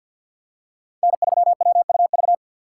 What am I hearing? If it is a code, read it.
Morse code, heard as a keyed tone, about 51 wpm, 700 Hz. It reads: N3WUV
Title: USA Call Signs - 5 Characters - 2x Character Spacing - 50wpm